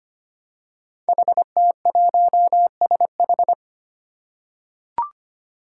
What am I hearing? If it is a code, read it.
Morse code, heard as a keyed tone, about 25 wpm, 700 Hz. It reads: HT1SH
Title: Five Challenging Letter-Number Combinations 25wpm